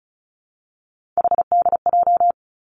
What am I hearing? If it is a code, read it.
Morse code, heard as a keyed tone, about 35 wpm, 700 Hz. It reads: HDJ
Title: Three Challenging Letter-Number Combinations 35wpm